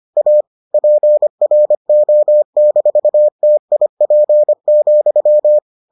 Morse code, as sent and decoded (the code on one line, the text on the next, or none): .- / .--. .-. --- -....- - .. .--. --..--
A PRO-TIP,